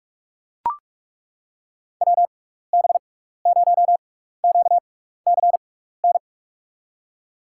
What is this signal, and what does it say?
Morse code, heard as a keyed tone, about 42 wpm, spaced as for 15 wpm, 700 Hz. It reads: WB0QCN
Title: USA Call Signs - 6x Character Spacing - 45wpm